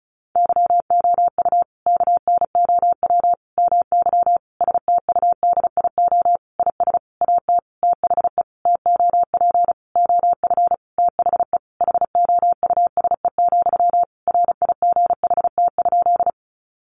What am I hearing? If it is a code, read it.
Morse code, heard as a keyed tone, about 34 wpm, 700 Hz. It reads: YOU KNOW MY STUDIO IS AT THE TOP OF THE HOUSE, RIGHT?
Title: Sentences from Top 1200 Words 35wpm